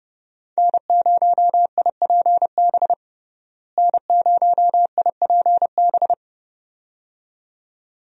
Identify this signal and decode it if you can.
Morse code, heard as a keyed tone, about 30 wpm, 700 Hz. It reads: N0IPB N0IPB
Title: USA Call Signs - 5 Characters - Repeated 2x - 30wpm